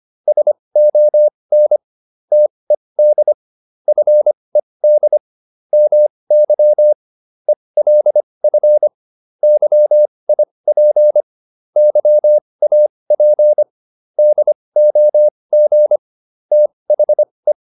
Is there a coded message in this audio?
Yes: Morse code from 0.3 to 17.5 s, heard as a keyed tone, about 25 wpm, spaced as for 20 wpm, 600 Hz. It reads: SON TED FED MY ELF YIP YAP DOG THE